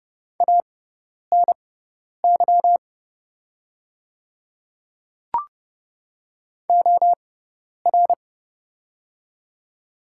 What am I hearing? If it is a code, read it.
Morse code, heard as a keyed tone, about 29 wpm, spaced as for 10 wpm, 700 Hz. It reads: ANY OR
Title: 6x Character Spacing - Top 200 Words - 30wpm